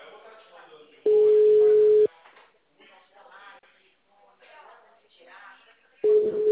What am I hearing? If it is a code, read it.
Ringback tone (Europe, 425 Hz), with speech.